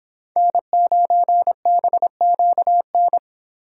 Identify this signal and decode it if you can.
Morse code, heard as a keyed tone, about 25 wpm, 700 Hz. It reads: N9BQN
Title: Speed Racing - USA 1x3 Call Signs 22wpm